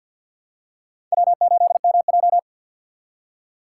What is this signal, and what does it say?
Morse code, heard as a keyed tone, about 53 wpm, 700 Hz. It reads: W8MJ
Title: USA Call Signs - 4 Characters - 2x Character Spacing - 50wpm